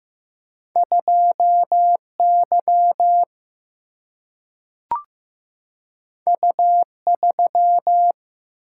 Morse code, heard as a keyed tone, about 15 wpm, 700 Hz. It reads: 2 Y U 3
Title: Single-Letter Single-Number 15wpm